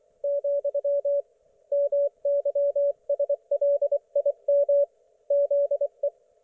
Morse code, heard as a keyed tone, about 24 wpm, 570 Hz. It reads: , MYSLIM ZE
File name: Archiv / CUC224 / ok1fhi cul224